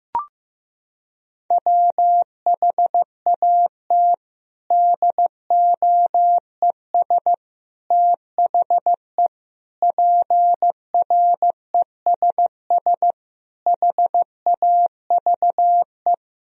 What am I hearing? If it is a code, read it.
Morse code, heard as a keyed tone, about 15 wpm, 700 Hz. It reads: WHAT DOES THE PRESS HAVE
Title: Sentences from Top 1000 Words - Review Part 3 - 15wpm